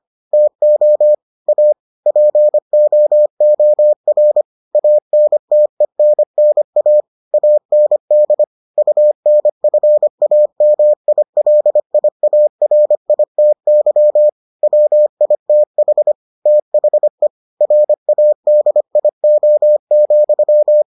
Morse code, heard as a keyed tone, about 25 wpm, 600 Hz. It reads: TO A POOR ANTENNA AND UNFAMILIARITY WITH THE RADIO,